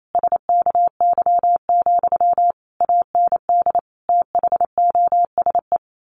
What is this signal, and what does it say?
Morse code, heard as a keyed tone, about 28 wpm, 700 Hz. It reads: SKY, AND THOSE
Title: A Sound Of Thunder 28wpm